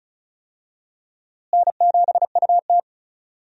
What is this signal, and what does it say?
Morse code, heard as a keyed tone, about 36 wpm, 700 Hz. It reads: N7UT
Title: USA Call Signs - 4 Characters - Repeated 2x - 35wpm